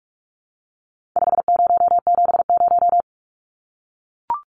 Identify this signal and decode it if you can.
Morse code, heard as a keyed tone, about 44 wpm, 700 Hz. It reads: 5070
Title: Numbers - 4-Digit 45wpm